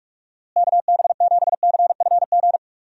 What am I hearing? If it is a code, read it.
Morse code, heard as a keyed tone, about 46 wpm, 700 Hz. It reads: KB7CFG
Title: USA Call Signs - 6 Characters - Repeated 3x - 45wpm